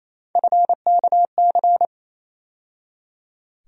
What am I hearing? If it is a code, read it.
Morse code, heard as a keyed tone, about 28 wpm, 700 Hz. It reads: FKC
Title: USA Call Sign Postfixes - 3 Letters 28wpm